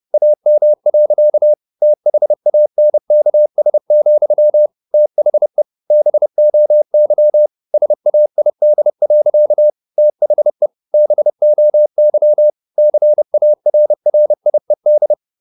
Morse code, heard as a keyed tone, about 30 wpm, 600 Hz. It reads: AM. THANKS, THE BOY SAID. THE BOY CARRIED